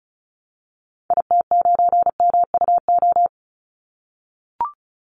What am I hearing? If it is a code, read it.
Morse code, heard as a keyed tone, about 34 wpm, 700 Hz. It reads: IT9MUO
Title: Call Signs - 35wpm